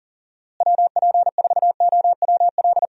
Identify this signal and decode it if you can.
Morse code, heard as a keyed tone, about 40 wpm, 700 Hz. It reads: WP4OWL
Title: USA Call Signs - 6 Characters - Repeated 3x - 40wpm